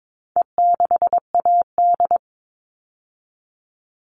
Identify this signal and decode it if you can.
Morse code, heard as a keyed tone, about 22 wpm, 700 Hz. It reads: E6AD
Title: All Cut Numbers - 4-Digit 22wpm